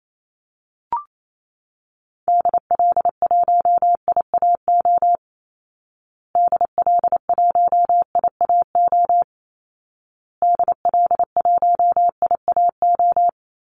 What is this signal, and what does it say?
Morse code, heard as a keyed tone, about 28 wpm, 700 Hz. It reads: DL1IAO DL1IAO DL1IAO
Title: Call Signs Repeated 2x - 28wpm